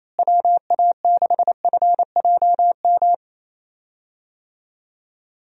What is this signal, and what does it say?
Morse code, heard as a keyed tone, about 28 wpm, 700 Hz. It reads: WA6FJM
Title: USA Call Signs - 6 Characters - Repeated 2x - 28wpm